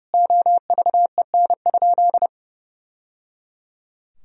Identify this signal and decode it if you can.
Morse code, heard as a keyed tone, about 30 wpm, 700 Hz. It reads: OVEN?
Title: Sentences from Top 1000 Words - Review Part 4 - 30wpm